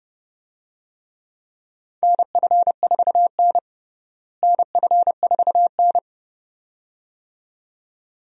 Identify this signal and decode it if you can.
Morse code, heard as a keyed tone, about 29 wpm, 700 Hz. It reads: NF4N NF4N
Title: USA Call Signs - 4 Characters - Repeated 2x - 30wpm